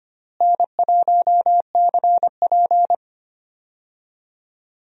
Morse code, heard as a keyed tone, about 25 wpm, 700 Hz. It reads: N1CP
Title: USA Call Signs - 4 Characters - Repeated 2x - 25wpm